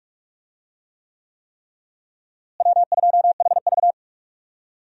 Morse code, heard as a keyed tone, about 46 wpm, 700 Hz. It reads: W2HV
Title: USA Call Signs - 4 Characters - Repeated 3x - 45wpm